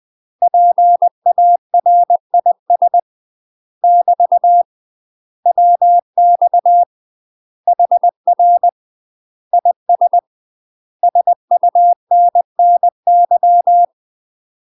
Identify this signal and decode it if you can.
Morse code, heard as a keyed tone, about 20 wpm, 700 Hz. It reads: PARIS <BT> WX HR IS SUNNY
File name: UK MASTER QSOs TRAINING/20WPM